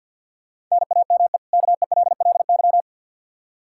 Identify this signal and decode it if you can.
Morse code, heard as a keyed tone, about 49 wpm, 700 Hz. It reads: NAME KELLY
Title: Practice QSO - Exchange 1 - Beginning - 50wpm